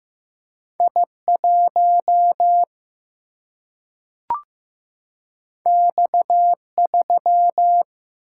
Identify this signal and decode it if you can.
Morse code, heard as a keyed tone, about 15 wpm, 700 Hz. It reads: I1 X3
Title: Single-Letter Single-Number 15wpm